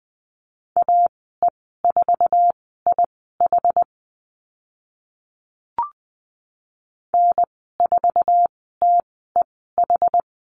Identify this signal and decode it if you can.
Morse code, heard as a keyed tone, about 20 wpm, spaced as for 14 wpm, 700 Hz. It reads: AE4IH N4TEH
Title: 062 Lesson 11g - Letter h - Call Signs 20wpm